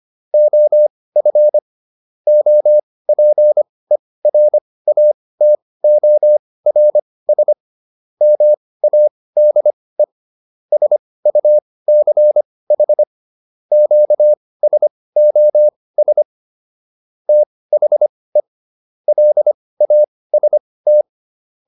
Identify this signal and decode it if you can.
Morse code, heard as a keyed tone, about 25 wpm, spaced as for 18 wpm, 600 Hz. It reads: OF OPERATORS MADE SUCH QSOS THE LAST